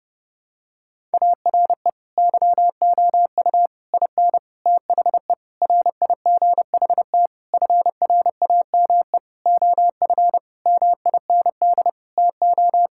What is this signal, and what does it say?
Morse code, heard as a keyed tone, about 30 wpm, 700 Hz. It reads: ARE YOU IN THE RIGHT FRAME OF MIND TO
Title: Sentences from Top 1000 Words - Review Part 6 - 30wpm